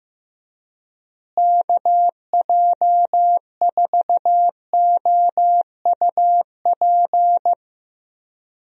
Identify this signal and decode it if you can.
Morse code, heard as a keyed tone, about 15 wpm, 700 Hz. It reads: KJ4OUP
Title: USA Call Signs - 6 Characters - Repeated 3x - 15wpm